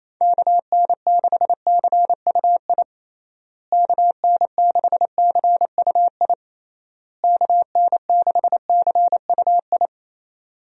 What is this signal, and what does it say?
Morse code, heard as a keyed tone, about 28 wpm, 700 Hz. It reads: KN6CUI KN6CUI KN6CUI
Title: USA Call Signs - 6 Characters - Repeated 3x - 28wpm